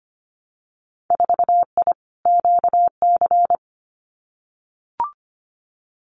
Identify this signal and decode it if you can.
Morse code, heard as a keyed tone, about 25 wpm, 700 Hz. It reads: 4I QC
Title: Winter Field Day Exchanges Encore 25wpm